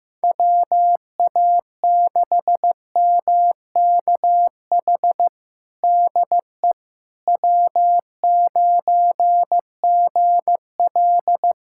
Morse code, heard as a keyed tone, about 15 wpm, 700 Hz. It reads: WA6MKH DE W9GL
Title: Practice QSO - Exchange 4 - Ending - 15wpm